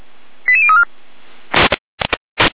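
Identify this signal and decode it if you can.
Five-tone selective call (ZVEI), heard as quick runs of five tones, tones of about 70 ms. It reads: 88034
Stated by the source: ZVEI five-tone call received on an SDR, logged by Feuerwehr Orlamünde.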